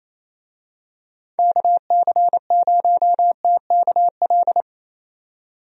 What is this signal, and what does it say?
Morse code, heard as a keyed tone, about 27 wpm, 700 Hz. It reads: KC0TKL
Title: USA Call Signs - 6 Characters - 28wpm